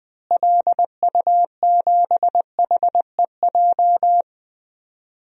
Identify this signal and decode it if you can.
Morse code, heard as a keyed tone, about 20 wpm, 700 Hz. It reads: LU7HEJ
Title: DXCC-100 Call Signs - 6 Characters - Repeated 3x - 20wpm